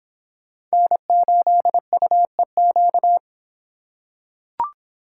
Morse code, heard as a keyed tone, about 25 wpm, 700 Hz. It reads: N8UEQ